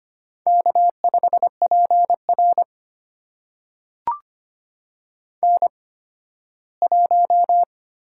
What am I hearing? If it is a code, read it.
Morse code, heard as a keyed tone, about 25 wpm, spaced as for 15 wpm, 700 Hz. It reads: K5PR N1
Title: USA Call Signs - 8x Character Spacing - 25wpm